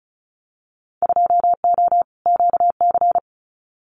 Morse code, heard as a keyed tone, about 34 wpm, 700 Hz. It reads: 2O QC
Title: Winter Field Day Exchanges 35wpm